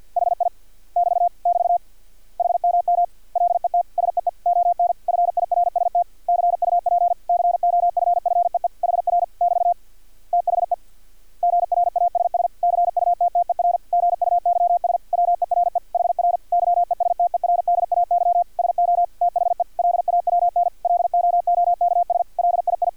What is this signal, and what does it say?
Morse code, heard as a keyed tone, about 50 wpm, 700 Hz. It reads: HI <BT> <BT> HMM LET SEE ON FIRST CUP COFFEE HR <BT> THE GRASS CUTTER GUYS WERE HR YESTERDAY SO THE LAWN LOOKS LII
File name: archived / CRSnet